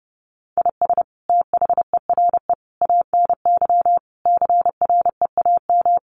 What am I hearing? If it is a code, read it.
Morse code, heard as a keyed tone, about 30 wpm, 700 Hz. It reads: IS THERE ANY CREAM
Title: Sentences from Top 1500 Words 30wpm